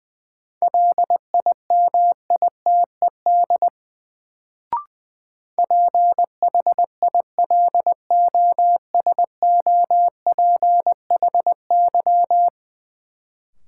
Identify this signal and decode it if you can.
Morse code, heard as a keyed tone, about 21 wpm, 700 Hz. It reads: LIMITED PHILOSOPHY
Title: Sentences from Top 900 Words - New Word Drill - 20wpm